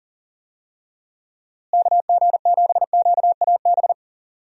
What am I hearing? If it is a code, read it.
Morse code, heard as a keyed tone, about 37 wpm, 700 Hz. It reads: KG7QAB
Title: USA Call Signs - 6 Characters - Repeated 3x - 40wpm